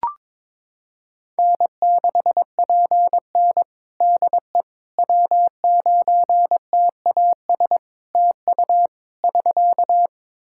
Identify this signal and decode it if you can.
Morse code, heard as a keyed tone, about 22 wpm, 700 Hz. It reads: N6PN DE W9TAS TU <SK>
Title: Practice QSO - Exchange 4 - Ending - 22wpm